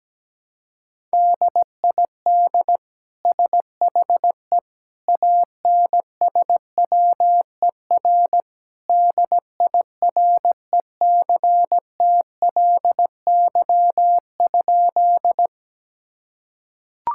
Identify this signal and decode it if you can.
Morse code, heard as a keyed tone, about 17 wpm, 700 Hz. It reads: DID SHE ANSWER DIRECTLY?
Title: Sentences from Top 1000 Words - Review Part 5 - 17wpm